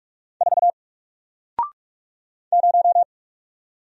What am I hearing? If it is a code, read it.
Morse code, heard as a keyed tone, about 44 wpm, 700 Hz. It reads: 4 0
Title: Single Character (Letter:Number:Punctuation:Prosign) 45wpm